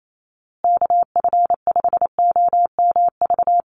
Morse code, heard as a keyed tone, about 28 wpm, 700 Hz. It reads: KF5OMV
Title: USA Call Signs Repeated 2x 28wpm